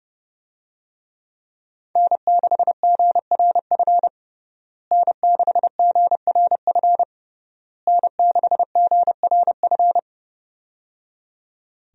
Morse code, heard as a keyed tone, about 30 wpm, 700 Hz. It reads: N6GRF N6GRF N6GRF
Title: USA Call Signs - 5 Characters - Repeated 3x - 30wpm